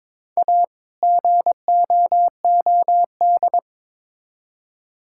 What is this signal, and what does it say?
Morse code, heard as a keyed tone, about 22 wpm, 700 Hz. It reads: A GOOD
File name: Build A Phrase - 2-Word Phrases - 22wpm